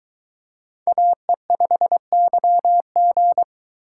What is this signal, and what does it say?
Morse code, heard as a keyed tone, about 22 wpm, 700 Hz. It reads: AE5YG